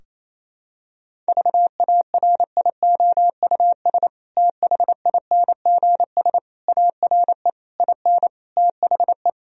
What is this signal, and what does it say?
Morse code, heard as a keyed tone, about 28 wpm, 700 Hz. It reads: VARIOUS THINGS ARE IN THE
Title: Sentences from Top 1000 Words - Review Part 4 - 28wpm